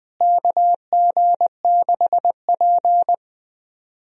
Morse code, heard as a keyed tone, about 20 wpm, 700 Hz. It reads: KG6P
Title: USA Call Signs - 4 Characters - Repeated 2x - 20wpm